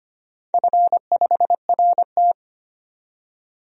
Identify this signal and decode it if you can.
Morse code, heard as a keyed tone, about 25 wpm, 700 Hz. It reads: F5RT
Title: DXCC-100 Call Signs - 4 Characters - 25wpm